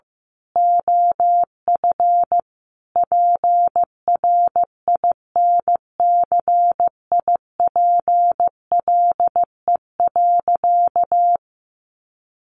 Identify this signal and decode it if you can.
Morse code, heard as a keyed tone, about 14 wpm, 700 Hz. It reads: OF PRINCIPLE.
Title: Sentences from Top 1400 Words 15wpm